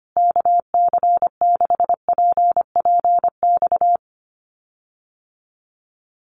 Morse code, heard as a keyed tone, about 25 wpm, 700 Hz. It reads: KC6PPX
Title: USA Call Signs Repeated 3x 25wpm